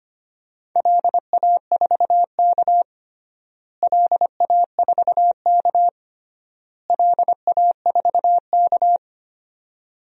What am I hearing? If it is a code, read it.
Morse code, heard as a keyed tone, about 25 wpm, 700 Hz. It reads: LA4K LA4K LA4K
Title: DXCC-100 Call Signs - 4 Characters - Repeated 3x - 25wpm